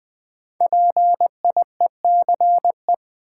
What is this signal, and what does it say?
Morse code, heard as a keyed tone, about 21 wpm, 700 Hz. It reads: PIECE
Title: Sets of 3 Words - New Word Drill - 20wpm